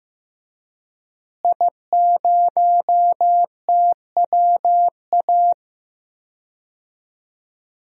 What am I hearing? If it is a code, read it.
Morse code, heard as a keyed tone, about 15 wpm, 700 Hz. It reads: I0TWA
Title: DXCC-100 Call Signs - 5 Characters - Repeated 3x - 15wpm